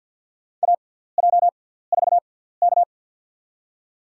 Morse code, heard as a keyed tone, about 47 wpm, spaced as for 17 wpm, 690 Hz. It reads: AJ4K
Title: USA Call Signs - 4 Characters - 6x Character Spacing - 50wpm